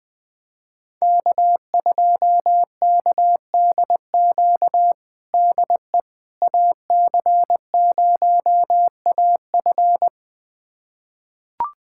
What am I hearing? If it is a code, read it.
Morse code, heard as a keyed tone, about 21 wpm, 700 Hz. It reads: K2KDQ DE AC0AF
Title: USA Callsign DE Callsign - 5 Characters - 20wpm